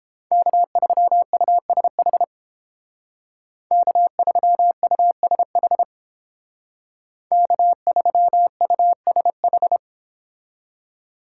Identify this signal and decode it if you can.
Morse code, heard as a keyed tone, about 28 wpm, 700 Hz. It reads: K3USH K3USH K3USH